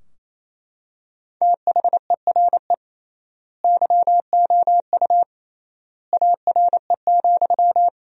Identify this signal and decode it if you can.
Morse code, heard as a keyed tone, about 28 wpm, 700 Hz. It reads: THERE YOU ARE,
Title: 3x Word-Spacing - Sentences from Top 100 Words - 28wpm